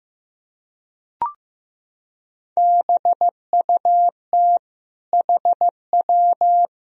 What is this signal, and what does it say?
Morse code, heard as a keyed tone, about 15 wpm, 700 Hz. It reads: BUT HW
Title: Sentences from Top 400 Words and Abbreviations 15wpm